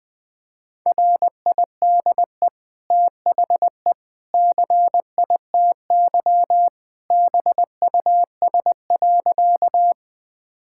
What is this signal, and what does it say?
Morse code, heard as a keyed tone, about 21 wpm, 700 Hz. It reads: RIDE THE CITY BUS.
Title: Easy Sentences with Hints - 20wpm